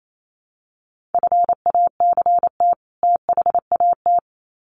Morse code, heard as a keyed tone, about 28 wpm, 700 Hz. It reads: FACT THAT
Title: Sets of 2 Words 28wpm